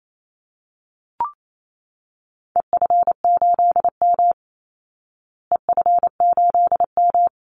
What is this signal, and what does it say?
Morse code, heard as a keyed tone, about 28 wpm, 700 Hz. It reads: EF8M EF8M
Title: Call Signs Repeated 3x - 28wpm